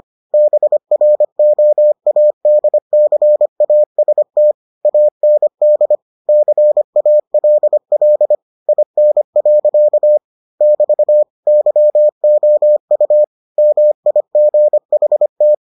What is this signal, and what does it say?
Morse code, heard as a keyed tone, about 25 wpm, 600 Hz. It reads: BROADCAST AND CALL IN. <BT> YOU MIGHT